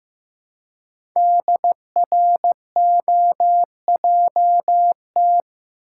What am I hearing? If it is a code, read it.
Morse code, heard as a keyed tone, about 15 wpm, 700 Hz. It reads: DROJT
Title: Code Groups - 5 Letters - 15wpm